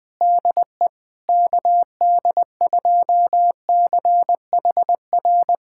Morse code, heard as a keyed tone, about 20 wpm, 700 Hz. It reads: DE KD2CHR
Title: USA Callsign DE Callsign - 6 Characters - 20wpm